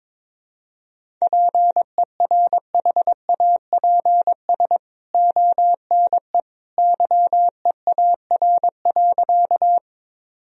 Morse code, heard as a keyed tone, about 22 wpm, 700 Hz. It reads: PERHAPS ONE YEAR.
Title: Sentences from Top 1000 Words - Review Part 1 - 22wpm